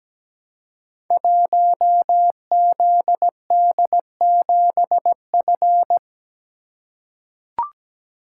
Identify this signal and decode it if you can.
Morse code, heard as a keyed tone, about 17 wpm, 700 Hz. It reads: 1ZD7F
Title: Code Groups - 5 Letter:Numbers - 17wpm